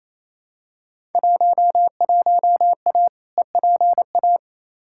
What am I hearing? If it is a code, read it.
Morse code, heard as a keyed tone, about 28 wpm, 700 Hz. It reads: 11A EPA
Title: Field Day Exchanges Encore - 28wpm